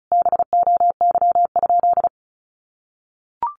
Morse code, heard as a keyed tone, about 34 wpm, 700 Hz. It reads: BOY?
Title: Sentences from Top 900 Words 35wpm